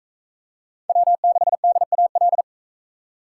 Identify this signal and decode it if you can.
Morse code, heard as a keyed tone, about 41 wpm, 700 Hz. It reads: W6DAL